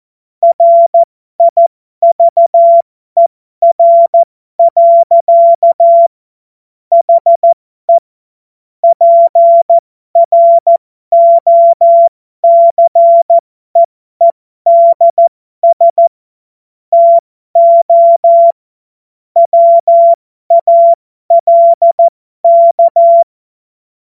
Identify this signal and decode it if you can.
Morse code, heard as a keyed tone, about 14 wpm, spaced as for 12 wpm, 680 Hz. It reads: RIVER. HE PROCEEDS TO WALK
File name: Morsekurs / Witze